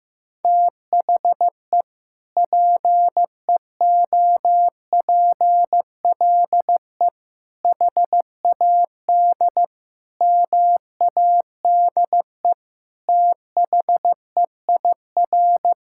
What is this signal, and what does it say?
Morse code, heard as a keyed tone, about 15 wpm, 700 Hz. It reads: THE PEOPLE HAD MADE THEIR
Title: Sentences from Top 1000 Words - Review Part 6 - 15wpm